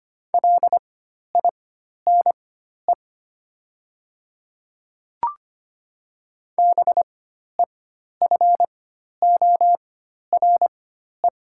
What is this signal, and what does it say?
Morse code, heard as a keyed tone, about 25 wpm, spaced as for 12 wpm, 700 Hz. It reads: LINE BEFORE
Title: 4x Character Spacing - Top 200 Words - 25wpm